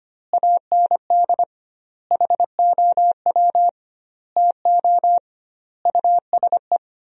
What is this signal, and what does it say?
Morse code, heard as a keyed tone, about 25 wpm, 700 Hz. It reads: AND HOW TO USE
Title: 2x Word-Spacing - Sentences from Top 500 Words - 25wpm